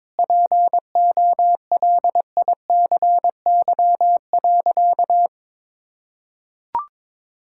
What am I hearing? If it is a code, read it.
Morse code, heard as a keyed tone, about 21 wpm, 700 Hz. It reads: POLICY.
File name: practice / 22wpm / Sentences with 8-Word Phrases - 22wpm